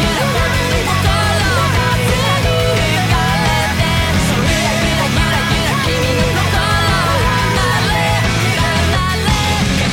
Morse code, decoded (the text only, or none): none